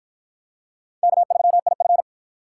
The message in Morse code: K3IF

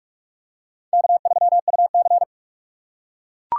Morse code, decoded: K3UC

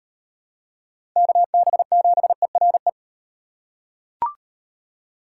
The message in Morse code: KB7ERE

